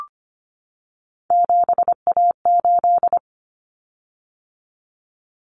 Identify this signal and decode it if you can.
Morse code, heard as a keyed tone, about 25 wpm, 700 Hz. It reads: 7A8
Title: Common Cut Numbers - 3-Digit 25wpm